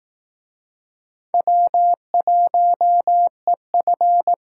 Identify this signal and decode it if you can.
Morse code, heard as a keyed tone, about 19 wpm, 700 Hz. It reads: W1EF